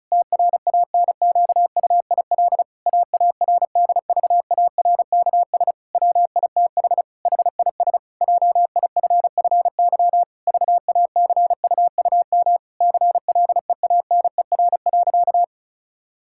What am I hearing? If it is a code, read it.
Morse code, heard as a keyed tone, about 36 wpm, 700 Hz. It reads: TRANQUIL AARDVARKS WITH HIS JIFFY VACUUM CLEANER.
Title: Pangram Challenge v2 - 35wpm